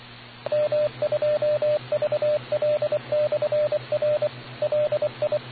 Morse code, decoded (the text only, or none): M2VL/R LI